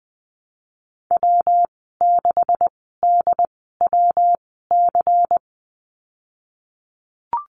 Morse code, heard as a keyed tone, about 20 wpm, spaced as for 14 wpm, 700 Hz. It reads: W6DWC
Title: 147 Lesson 24g - Letter y - Call Signs 20wpm